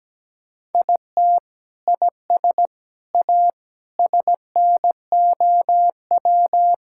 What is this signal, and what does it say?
Morse code, heard as a keyed tone, about 17 wpm, 700 Hz. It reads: IT IS A SNOW
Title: Sentences from Top 1600 Words 17wpm